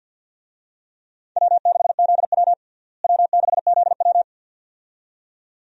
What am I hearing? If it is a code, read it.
Morse code, heard as a keyed tone, about 50 wpm, 700 Hz. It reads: W6ZW W6ZW
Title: USA Call Signs - 4 Characters - Repeated 2x - 50wpm